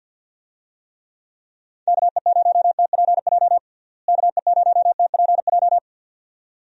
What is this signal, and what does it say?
Morse code, heard as a keyed tone, about 51 wpm, 700 Hz. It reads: KE0TPJ KE0TPJ
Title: USA Call Signs - 6 Characters - Repeated 2x - 50wpm